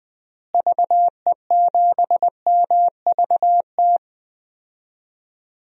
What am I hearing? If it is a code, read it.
Morse code, heard as a keyed tone, about 20 wpm, 700 Hz. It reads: VE7MVT
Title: DXCC-100 Call Signs - 6 Characters - Repeated 2x - 20wpm